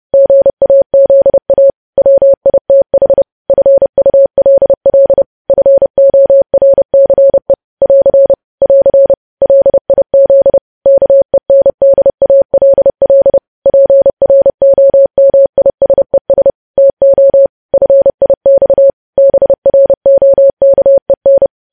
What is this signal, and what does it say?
Morse code, heard as a keyed tone, about 30 wpm, 550 Hz. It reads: GAZA WITH FULL FORCE <AR> <AR> LIZ KENDALL PROMISES TO FIX BROKEN
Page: news headlines in morse code at 30 wpm